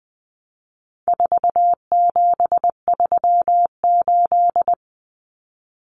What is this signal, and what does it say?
Morse code, heard as a keyed tone, about 20 wpm, 700 Hz. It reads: 4738
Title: Numbers - 4-Digit 20wpm